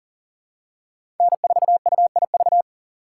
Morse code, heard as a keyed tone, about 40 wpm, 700 Hz. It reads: N4UIV
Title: USA Call Signs - 5 Characters - 4x Character Spacing - 40wpm